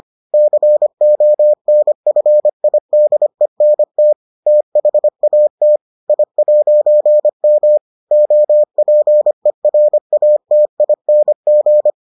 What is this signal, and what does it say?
Morse code, heard as a keyed tone, about 25 wpm, 600 Hz. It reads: CONFIDENT THAT I'M OPERATING